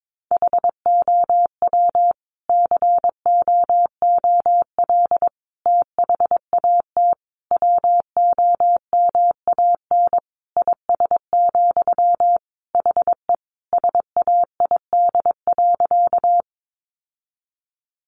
Morse code, heard as a keyed tone, about 22 wpm, 700 Hz. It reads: HOW COOL THAT WOMAN IS, HE SAID.
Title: Sentences from Top 1200 Words 22wpm